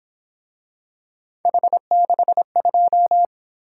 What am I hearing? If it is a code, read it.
Morse code, heard as a keyed tone, about 27 wpm, 700 Hz. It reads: H62
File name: Speed-Racing - Three Letter-Number Combinations - 22wpm